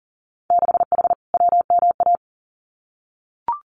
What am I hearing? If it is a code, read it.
Morse code, heard as a keyed tone, about 42 wpm, 700 Hz. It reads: 6H WMA
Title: Winter Field Day Exchanges Encore 40wpm